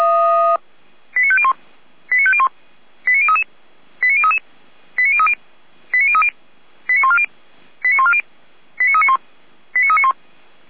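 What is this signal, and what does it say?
Five-tone selective call (ZVEI), heard as quick runs of five tones, tones of about 70 ms. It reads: 89501 89501 89033 89033 89030 89030 89140 89140 89391 89391
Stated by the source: ZVEI five-tone selective call 89125, received by SDR.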